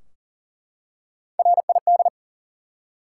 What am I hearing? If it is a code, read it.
Morse code, heard as a keyed tone, about 40 wpm, 700 Hz. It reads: RID
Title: Three Letter Words - 40wpm